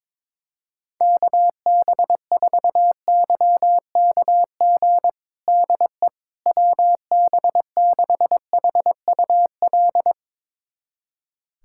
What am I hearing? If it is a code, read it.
Morse code, heard as a keyed tone, about 22 wpm, 700 Hz. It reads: KB4YKG DE WB6HUL
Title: USA Callsign DE Callsign 22wpm